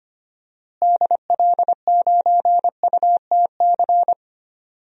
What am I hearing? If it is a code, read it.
Morse code, heard as a keyed tone, about 25 wpm, 700 Hz. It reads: DL9UTC
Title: DXCC-100 Call Signs - 6 Characters - Repeated 3x - 25wpm